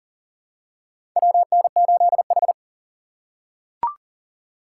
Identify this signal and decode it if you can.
Morse code, heard as a keyed tone, about 40 wpm, 700 Hz. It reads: WN8H
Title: USA Call Signs - 4 Characters - 4x Character Spacing - 40wpm